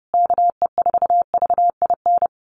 Morse code, heard as a keyed tone, about 30 wpm, 700 Hz. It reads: KE4VIN
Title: USA Call Signs Repeated 3x 30wpm